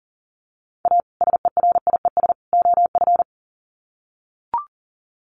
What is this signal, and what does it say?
Morse code, heard as a keyed tone, about 37 wpm, 700 Hz. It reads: A SERIES OF
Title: Sets of 3 Words 40wpm